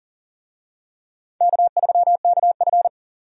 Morse code, heard as a keyed tone, about 40 wpm, 700 Hz. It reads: K3KF
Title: USA Call Signs - 4 Characters - 40wpm